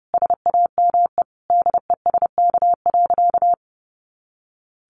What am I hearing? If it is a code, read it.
Morse code, heard as a keyed tone, about 28 wpm, 700 Hz. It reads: SAME DESK.
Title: A Sound Of Thunder 30wpm